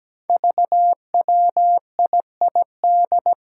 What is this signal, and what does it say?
Morse code, heard as a keyed tone, about 17 wpm, 700 Hz. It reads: VWIID
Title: Five Challenging Letter-Number Combinations 17wpm